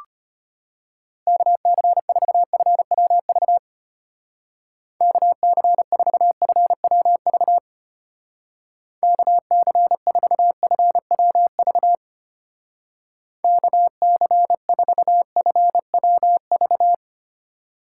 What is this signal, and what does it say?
Morse code, heard as a keyed tone, about 32 wpm, 700 Hz. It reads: KC4FWV KC4FWV KC4FWV KC4FWV